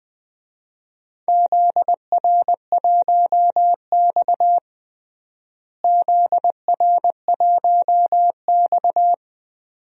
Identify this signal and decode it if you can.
Morse code, heard as a keyed tone, about 20 wpm, 700 Hz. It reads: ZR1X ZR1X
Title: DXCC-100 Call Signs - 4 Characters - Repeated 2x - 20wpm